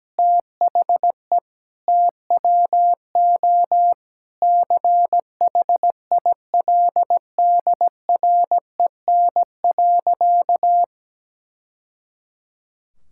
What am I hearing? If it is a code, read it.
Morse code, heard as a keyed tone, about 17 wpm, 700 Hz. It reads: THE TWO CHILDREN.
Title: Sentences from Top 1000 Words - Review Part 5 - 17wpm